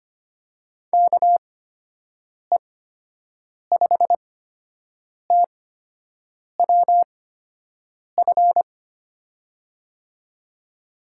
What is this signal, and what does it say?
Morse code, heard as a keyed tone, about 25 wpm, spaced as for 7 wpm, 700 Hz. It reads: KE5TWF
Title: USA Call Signs - 6 Characters - 8x Character Spacing - 25wpm